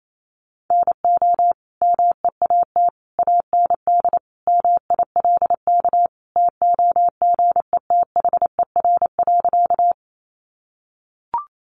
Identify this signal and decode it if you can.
Morse code, heard as a keyed tone, about 28 wpm, 700 Hz. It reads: NO MEAT AND MILK TOGETHER.
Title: Sentences from Top 1500 Words 28wpm